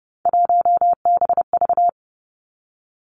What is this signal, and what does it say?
Morse code, heard as a keyed tone, about 30 wpm, 700 Hz. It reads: 1BV
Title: Three Challenging Letter-Number Combinations 30wpm